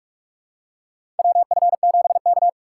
Morse code, heard as a keyed tone, about 44 wpm, 700 Hz. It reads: WF7K
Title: USA Call Signs - 3x Character Spacing - 45wpm